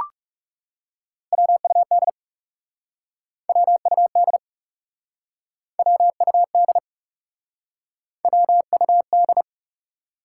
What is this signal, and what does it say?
Morse code, heard as a keyed tone, about 40 wpm, 700 Hz. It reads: WUD WUD WUD WUD